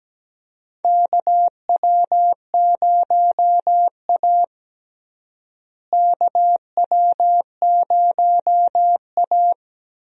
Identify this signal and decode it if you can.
Morse code, heard as a keyed tone, about 17 wpm, 700 Hz. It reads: KW0A KW0A
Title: USA Call Signs - 4 Characters - Repeated 3x - 17wpm